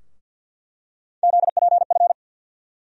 Morse code, heard as a keyed tone, about 47 wpm, 700 Hz. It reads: ZPF
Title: USA Call Sign Postfixes - 3 Letters 50wpm